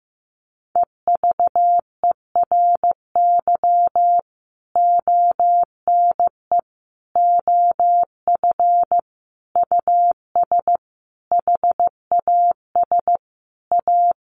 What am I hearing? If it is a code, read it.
Morse code, heard as a keyed tone, about 15 wpm, 700 Hz. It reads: EVERY ONE OF US HAS A
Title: Sentences from Top 1500 Words 15wpm